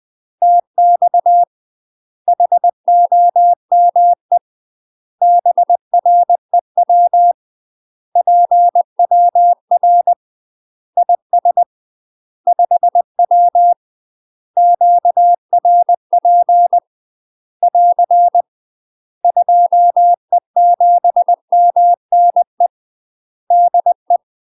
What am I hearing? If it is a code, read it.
Morse code, heard as a keyed tone, about 20 wpm, 700 Hz. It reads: TX HOME BREW PWR IS 5W QRP <AR> 2E7MNE DE
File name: UK MASTER QSOs TRAINING / 20WPM / UK MASTER QSOs 0053